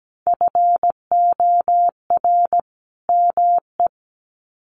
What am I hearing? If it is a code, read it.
Morse code, heard as a keyed tone, about 17 wpm, 700 Hz. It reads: FOR ME
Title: Sets of 2 Words 17wpm